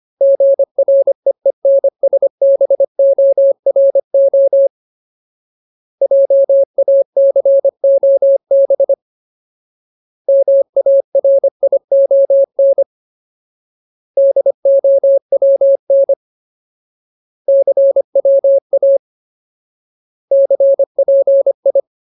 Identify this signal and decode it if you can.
Morse code, heard as a keyed tone, about 25 wpm, 550 Hz. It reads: GREENSBORO JACOB MARION DOWN CWA CPI
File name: Ultimate Word List / 25WPM Ultimate